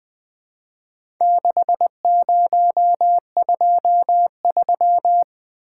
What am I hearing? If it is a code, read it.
Morse code, heard as a keyed tone, about 20 wpm, 700 Hz. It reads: 6023
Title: Four Letter-Number Combinations - 20wpm